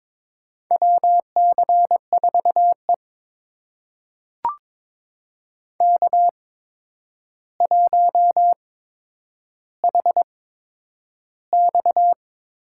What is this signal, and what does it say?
Morse code, heard as a keyed tone, about 22 wpm, spaced as for 9 wpm, 700 Hz. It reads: WC4E K1HX
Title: USA Call Signs - 4 Characters - 8x Character Spacing - 22wpm